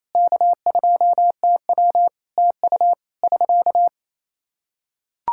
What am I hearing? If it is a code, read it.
Morse code, heard as a keyed tone, about 27 wpm, 700 Hz. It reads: K2TW TU <SK>
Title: Practice QSO - Exchange 4 - Ending - 28wpm